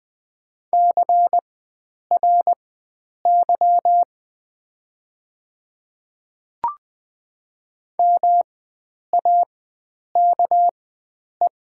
Morse code, heard as a keyed tone, about 20 wpm, spaced as for 9 wpm, 700 Hz. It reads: CRY MAKE
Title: 4x Character Spacing - Top 500 Words - 20wpm